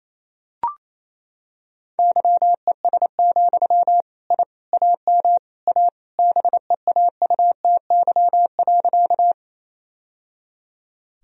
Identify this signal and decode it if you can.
Morse code, heard as a keyed tone, about 27 wpm, 700 Hz. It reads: YES, I AM A BEAUTY.
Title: Sentences from Top 1000 Words - Review Part 2 - 28wpm